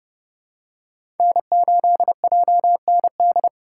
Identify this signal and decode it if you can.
Morse code, heard as a keyed tone, about 30 wpm, 700 Hz. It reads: N8JND
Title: USA Call Signs - 2x Character Spacing - 30wpm